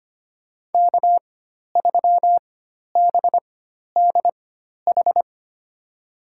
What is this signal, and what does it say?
Morse code, heard as a keyed tone, about 25 wpm, spaced as for 12 wpm, 700 Hz. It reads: K3BDH
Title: USA Call Signs - 5 Characters - 4x Character Spacing - 25wpm